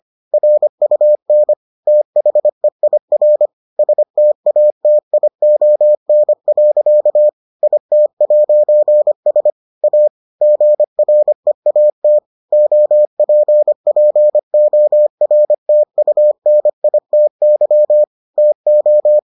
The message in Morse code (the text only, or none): RUN THEIR STATION. IT'S A GREAT OPPORTUNITY TO